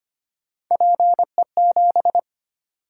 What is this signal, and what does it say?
Morse code, heard as a keyed tone, about 25 wpm, 700 Hz. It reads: PE7
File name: Three Letter-Number Combinations - 25wpm